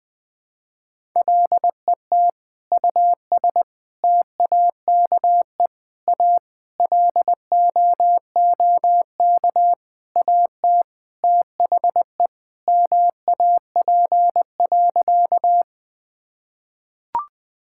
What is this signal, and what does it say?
Morse code, heard as a keyed tone, about 21 wpm, 700 Hz. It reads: LET US TAKE A LOOK AT THE MAP.
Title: Sentences with 5-Word Phrases - Spoken Beginning - 20wpm